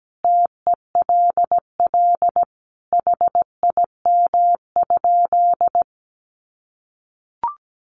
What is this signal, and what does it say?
Morse code, heard as a keyed tone, about 17 wpm, 700 Hz. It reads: TELL HIM?
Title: Sentences from Top 200 Words 17wpm